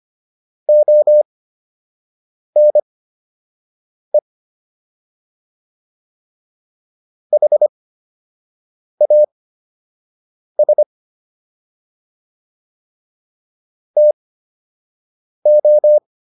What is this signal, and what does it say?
Morse code, heard as a keyed tone, about 25 wpm, spaced as for 6 wpm, 600 Hz. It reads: ONE HAS TO